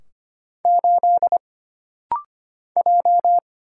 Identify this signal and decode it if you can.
Morse code, heard as a keyed tone, about 25 wpm, 700 Hz. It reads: 8 J
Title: Single Letter-Number - Rapid-Fire 25wpm